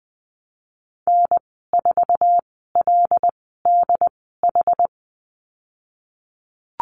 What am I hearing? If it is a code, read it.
Morse code, heard as a keyed tone, about 20 wpm, spaced as for 15 wpm, 700 Hz. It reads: N4LDH
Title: 082 Lesson 14g - Number 2 - Call Signs 20wpm